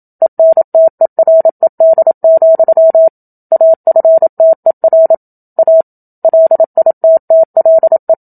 Morse code, heard as a keyed tone, about 27 wpm, 650 Hz. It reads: ENTERED, AFTER A LITTLE
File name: Book Two / 01-Chapter 1 - 27 WPM